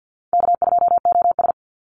Morse code, heard as a keyed tone, about 50 wpm, 700 Hz. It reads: K2OS
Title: Easy Call Signs - 50wpm